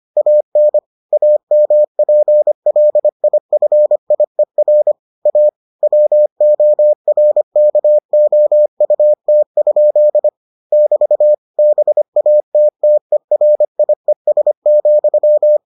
Morse code, heard as a keyed tone, about 25 wpm, 600 Hz. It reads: AN AMPLIFIER A WORKOUT? <BT> BATTERIES,